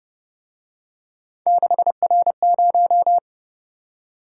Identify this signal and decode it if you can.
Morse code, heard as a keyed tone, about 30 wpm, 700 Hz. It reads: 6R0
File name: Speed-Racing - Three Letter-Number Combinations - 25wpm